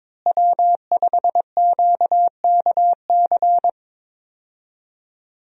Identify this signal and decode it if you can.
Morse code, heard as a keyed tone, about 22 wpm, 700 Hz. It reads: W5QKC
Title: USA Call Signs - 5 Characters - Repeated 3x - 22wpm